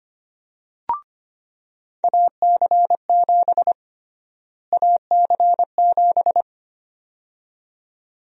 Morse code, heard as a keyed tone, about 25 wpm, 700 Hz. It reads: AC7 AC7
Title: USA Call Sign Prefixes - Repeated 2x - 25wpm